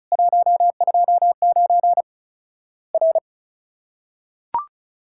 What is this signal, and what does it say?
Morse code, heard as a keyed tone, about 36 wpm, 700 Hz. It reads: 129 R
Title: Worked All Europe DX Contest - QTC Exchange - 35wpm